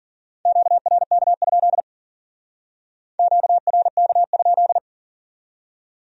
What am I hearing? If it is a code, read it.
Morse code, heard as a keyed tone, about 49 wpm, 700 Hz. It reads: QRK? QRK?
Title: Speed-Racing Q-Codes - 40wpm